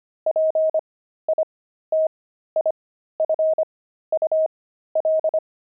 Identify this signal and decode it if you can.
Morse code, heard as a keyed tone, about 25 wpm, spaced as for 13 wpm, 630 Hz. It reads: PITIFUL
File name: pangram-fs-13-cs-25-crazy